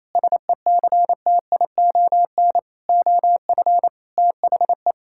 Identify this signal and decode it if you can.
Morse code, heard as a keyed tone, about 28 wpm, 700 Hz. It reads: SECTION OF THE
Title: Sentences from Top 1000 Words - Review Part 5 - 28wpm